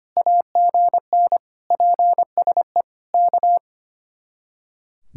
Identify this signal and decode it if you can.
Morse code, heard as a keyed tone, about 32 wpm, 700 Hz. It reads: AGN PSE K